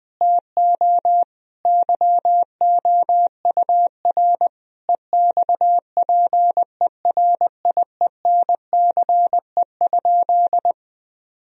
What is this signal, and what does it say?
Morse code, heard as a keyed tone, about 21 wpm, 700 Hz. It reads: TO YOUR EXPERIENCE?
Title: Sentences from Top 1000 Words - Review Part 4 - 20wpm